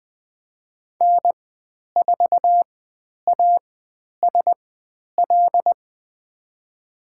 Morse code, heard as a keyed tone, about 20 wpm, spaced as for 10 wpm, 700 Hz. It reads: N4ASL
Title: 095 Lesson 16g - Letter c - Call Signs 20wpm